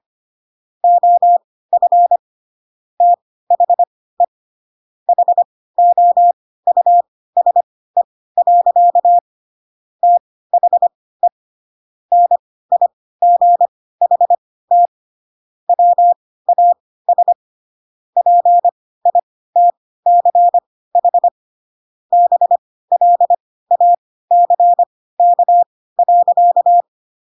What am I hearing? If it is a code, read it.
Morse code, heard as a keyed tone, about 25 wpm, spaced as for 16 wpm, 700 Hz. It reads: OF THE HOUSE. THE NIGHT WAS PITCH BLACK.